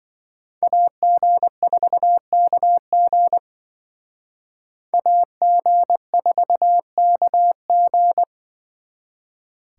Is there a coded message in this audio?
Yes: Morse code from 0.6 to 8.2 s, heard as a keyed tone, about 26 wpm, 700 Hz. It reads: AG4KG AG4KG